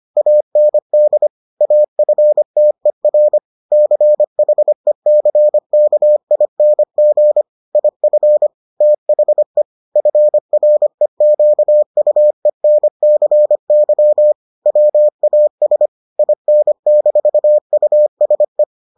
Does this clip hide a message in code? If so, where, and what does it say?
Yes: Morse code from 0.2 to 18.6 s, heard as a keyed tone, about 25 wpm, 600 Hz. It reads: AND AFTER CHECKING IF THE FREQUENCY WAS IN-USE